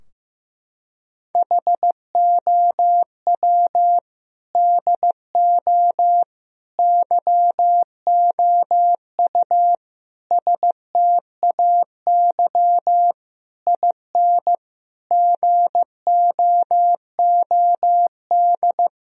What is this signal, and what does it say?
Morse code, heard as a keyed tone, about 15 wpm, 700 Hz. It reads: HOW DO YOU STAY IN GOOD
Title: Sentences from Top 1000 Words - Review Part 3 - 15wpm